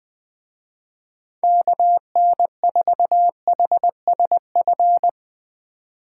Morse code, heard as a keyed tone, about 20 wpm, 700 Hz. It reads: KN4HSF
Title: Speed Racing - USA 2x3 Call Signs 17wpm